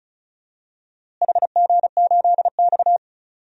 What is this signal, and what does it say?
Morse code, heard as a keyed tone, about 36 wpm, 700 Hz. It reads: HG8X
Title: DXCC-100 Call Signs - 4 Characters - Repeated 2x - 35wpm